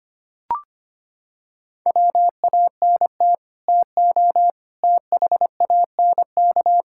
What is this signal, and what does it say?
Morse code, heard as a keyed tone, about 25 wpm, 700 Hz. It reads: WANT TO THANK
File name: Sets of 9 Words - 25wpm